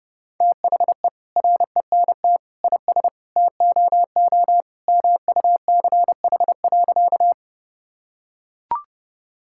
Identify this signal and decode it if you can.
Morse code, heard as a keyed tone, about 30 wpm, 700 Hz. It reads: THE RENT IS TOO MUCH.
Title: Sentences from Top 1000 Words - Review Part 5 - 30wpm